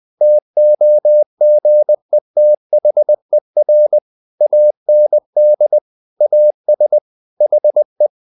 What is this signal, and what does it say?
Morse code, heard as a keyed tone, about 20 wpm, 600 Hz. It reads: TOGETHER AND AS HE
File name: the old man and the sea part 2 20wpm